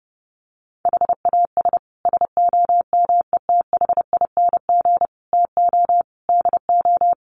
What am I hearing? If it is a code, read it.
Morse code, heard as a keyed tone, about 30 wpm, 700 Hz. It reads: HAS SOMETHING TO DO
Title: Sets of 5 Words Encore 30wpm